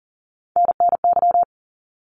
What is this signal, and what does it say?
Morse code, heard as a keyed tone, about 38 wpm, 700 Hz. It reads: NNY
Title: ARRL Sections 40wpm